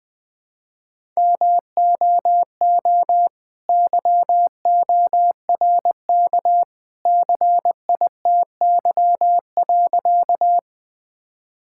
Morse code, heard as a keyed tone, about 20 wpm, 700 Hz. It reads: MOO YORK CITY.
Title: All-Age Appropriate Jokes - 20wpm